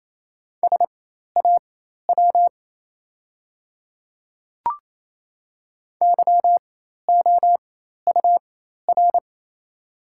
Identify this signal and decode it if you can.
Morse code, heard as a keyed tone, about 28 wpm, spaced as for 13 wpm, 700 Hz. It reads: SAW YOUR
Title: 4x Character Spacing - Top 300 Words - 28wpm